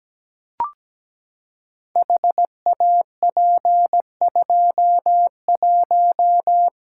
Morse code, heard as a keyed tone, about 17 wpm, 700 Hz. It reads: HAP21
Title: Code Groups - 5 Letter:Numbers - 17wpm